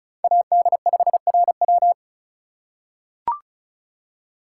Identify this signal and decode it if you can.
Morse code, heard as a keyed tone, about 36 wpm, 700 Hz. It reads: AD5RW
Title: USA Call Signs - 3x Character Spacing - 35wpm